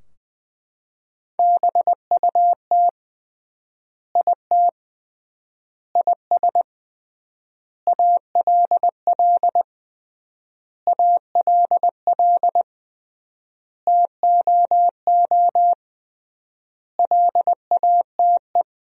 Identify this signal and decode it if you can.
Morse code, heard as a keyed tone, about 21 wpm, 700 Hz. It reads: BUT IT IS ALL ALL TOO LATE
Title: 3x Word-Spacing - Sentences from Top 300 Words - 20wpm